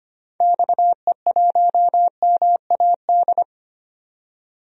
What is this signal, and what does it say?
Morse code, heard as a keyed tone, about 25 wpm, 700 Hz. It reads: XE1MAD
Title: DXCC-100 Call Signs - 6 Characters - 25wpm